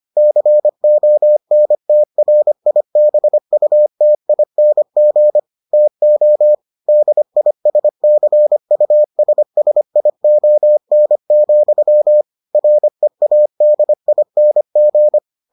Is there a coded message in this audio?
Yes: Morse code from 0.2 to 15.2 s, heard as a keyed tone, about 25 wpm, 600 Hz. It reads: CONTRIBUTING TO DISCUSSION, READING